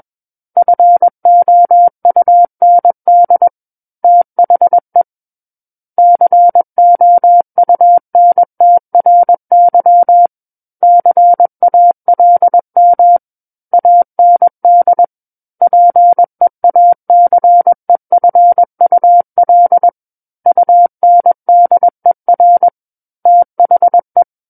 Morse code, heard as a keyed tone, about 21 wpm, 700 Hz. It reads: FOUND THE COUNTRY CALM AND PEACEFUL UNDER THE
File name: Book One / 12-Chapter 12 - 21 WPM